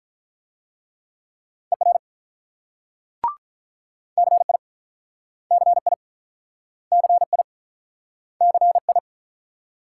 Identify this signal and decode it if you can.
Morse code, heard as a keyed tone, about 46 wpm, 700 Hz. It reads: ER CI CI CI CI